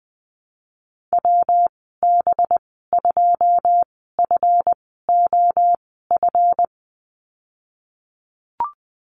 Morse code, heard as a keyed tone, about 20 wpm, spaced as for 14 wpm, 700 Hz. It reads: WB2FOF
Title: 200 Lesson 32g - Letter k - Call Signs 20wpm